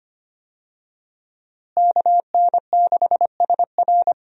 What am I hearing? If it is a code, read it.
Morse code, heard as a keyed tone, about 25 wpm, 700 Hz. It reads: KN6SR
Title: USA Call Signs - 5 Characters - Repeated 2x - 25wpm